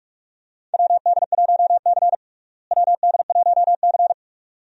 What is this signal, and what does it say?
Morse code, heard as a keyed tone, about 46 wpm, 700 Hz. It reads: WD1C WD1C
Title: USA Call Signs - 4 Characters - Repeated 2x - 45wpm